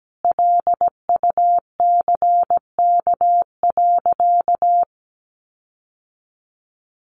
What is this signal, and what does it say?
Morse code, heard as a keyed tone, about 17 wpm, 700 Hz. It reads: LUCK.
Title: Sentences from Top 1500 Words 17wpm